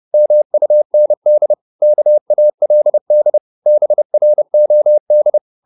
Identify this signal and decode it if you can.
Morse code, heard as a keyed tone, about 30 wpm, 600 Hz. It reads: MUND KALD BROD
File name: Korte ord DK 30wpm